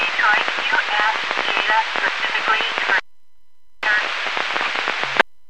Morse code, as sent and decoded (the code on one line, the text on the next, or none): none